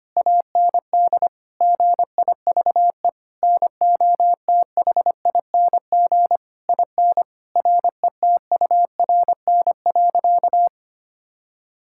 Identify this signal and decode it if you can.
Morse code, heard as a keyed tone, about 25 wpm, 700 Hz. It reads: AND GIVE NOTHING IN RETURN.
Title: Sentences from Top 1000 Words - Review Part 6 - 25wpm